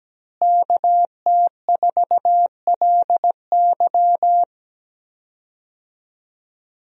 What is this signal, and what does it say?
Morse code, heard as a keyed tone, about 17 wpm, 700 Hz. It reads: KT4LY
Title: USA Call Signs - 5 Characters - Repeated 2x - 17wpm